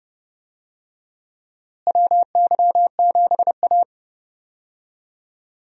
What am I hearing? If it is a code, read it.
Morse code, heard as a keyed tone, about 30 wpm, 700 Hz. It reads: WY7A
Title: USA Call Signs - 4 Characters - Repeated 3x - 30wpm